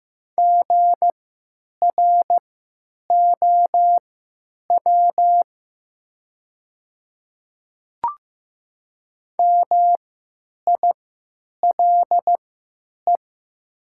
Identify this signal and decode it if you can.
Morse code, heard as a keyed tone, about 15 wpm, spaced as for 9 wpm, 700 Hz. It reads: GROW MILE
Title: 3x Character Spacing - Top 300 Words - 15wpm